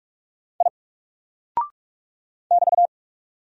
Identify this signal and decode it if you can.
Morse code, heard as a keyed tone, about 44 wpm, 700 Hz. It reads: I <BT>
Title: Single Character (Letter:Number:Punctuation:Prosign) 45wpm